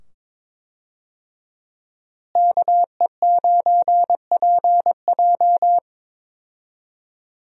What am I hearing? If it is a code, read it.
Morse code, heard as a keyed tone, about 22 wpm, 700 Hz. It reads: KE9PJ
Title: USA Call Signs - 5 Characters - Repeated 2x - 22wpm